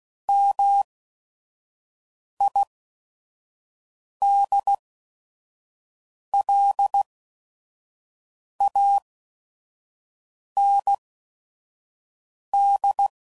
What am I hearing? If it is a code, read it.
Morse code, heard as a keyed tone, about 16 wpm, spaced as for 5 wpm, 800 Hz. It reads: MIDLAND
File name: CW Code Groups @5WPM / fromSM's / Cities